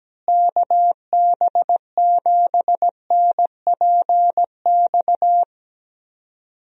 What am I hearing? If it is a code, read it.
Morse code, heard as a keyed tone, about 17 wpm, 700 Hz. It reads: KB7NPX
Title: USA Call Signs - 6 Characters - Repeated 3x - 17wpm